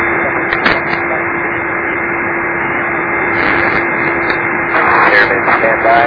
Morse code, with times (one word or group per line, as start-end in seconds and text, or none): none